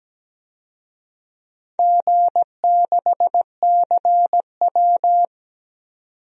Morse code, heard as a keyed tone, about 17 wpm, 700 Hz. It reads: G6CW
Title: DXCC-100 Call Signs - 4 Characters - 17wpm